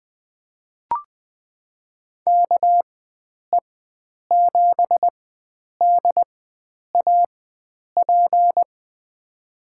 Morse code, heard as a keyed tone, about 20 wpm, spaced as for 9 wpm, 700 Hz. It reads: KE7DAP
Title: USA Call Signs - 6 Characters - 4x Character Spacing - 20wpm